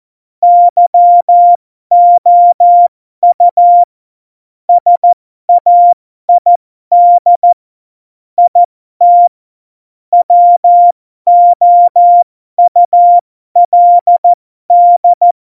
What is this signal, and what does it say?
Morse code, heard as a keyed tone, about 14 wpm, spaced as for 12 wpm, 700 Hz. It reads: YOU SAID IT WOULD
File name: Morsekurs / Witze